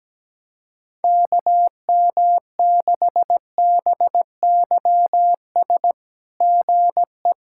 Morse code, heard as a keyed tone, about 17 wpm, 700 Hz. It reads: KM6BYS GE
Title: SOTA QSO Practice 17wpm